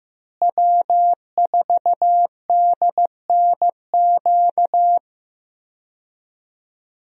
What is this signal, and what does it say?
Morse code, heard as a keyed tone, about 16 wpm, 700 Hz. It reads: W4DNQ
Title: USA Call Signs - 5 Characters - Repeated 3x - 15wpm